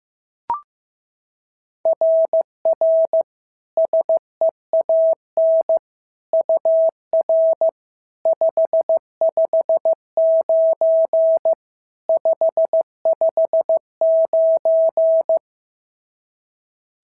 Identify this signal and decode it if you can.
Morse code, heard as a keyed tone, about 14 wpm, 650 Hz. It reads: RR SEAN UR 559 559
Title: SOTA QSO Practice 15wpm